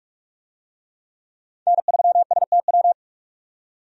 Morse code, heard as a keyed tone, about 43 wpm, 700 Hz. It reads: N3STW
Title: USA Call Signs - 5 Characters - 45wpm